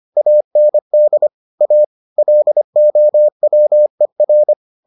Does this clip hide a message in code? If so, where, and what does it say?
Yes: Morse code from 0.2 to 4.5 s, heard as a keyed tone, about 25 wpm, 600 Hz. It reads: AND A LOWER